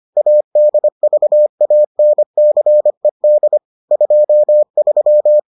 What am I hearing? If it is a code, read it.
Morse code, heard as a keyed tone, about 25 wpm, 600 Hz. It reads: ADVANCED 23